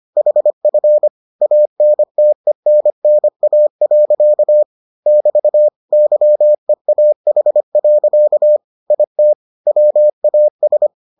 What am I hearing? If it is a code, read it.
Morse code, heard as a keyed tone, about 25 wpm, 600 Hz. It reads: HF ANTENNA. <BT> YEAH. IT WAS